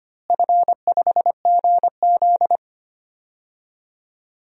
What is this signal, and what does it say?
Morse code, heard as a keyed tone, about 25 wpm, 700 Hz. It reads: F5GZ
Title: DXCC-100 Call Signs - 4 Characters - Repeated 2x - 25wpm